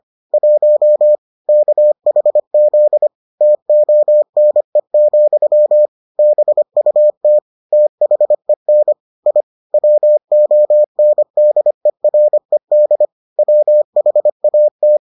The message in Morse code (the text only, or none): 1 KHZ TONE, BUT THEN I WONDERED WHAT